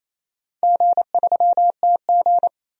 Morse code, heard as a keyed tone, about 28 wpm, 690 Hz. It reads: G3TG
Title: DXCC-100 Call Signs - 4 Characters - 28wpm